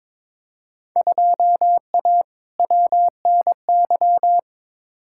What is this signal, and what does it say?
Morse code, heard as a keyed tone, about 22 wpm, 700 Hz. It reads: 2A WNY
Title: Field Day Exchanges - 22wpm